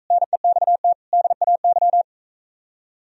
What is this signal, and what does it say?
Morse code, heard as a keyed tone, about 45 wpm, 700 Hz. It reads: NEXT DAY